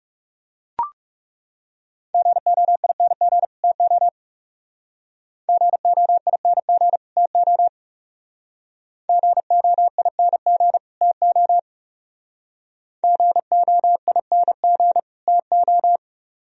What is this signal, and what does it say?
Morse code, heard as a keyed tone, about 37 wpm, 700 Hz. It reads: GOING TO GOING TO GOING TO GOING TO
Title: Speed-Racing - Sets of 2 Words 30wpm